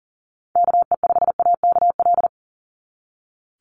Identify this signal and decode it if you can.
Morse code, heard as a keyed tone, about 40 wpm, 700 Hz. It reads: KE5AKL
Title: NA SOTA Activator Callsigns 40wpm